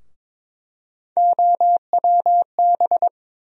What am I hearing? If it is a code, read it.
Morse code, heard as a keyed tone, about 21 wpm, 700 Hz. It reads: OWB
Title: USA Call Sign Postfixes - 3 Letters 22wpm